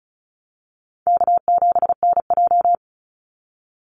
Morse code, heard as a keyed tone, about 34 wpm, 700 Hz. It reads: K7NJ
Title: Easy Call Signs Repeated 2x - 35wpm